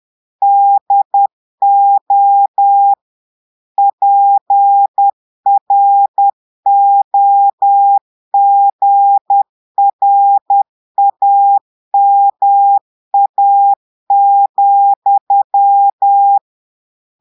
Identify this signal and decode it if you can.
Morse code, heard as a keyed tone, about 10 wpm, 800 Hz. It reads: DO PROGRAMA,
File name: portugais texte en clair 2 vitesse 10 mots minutes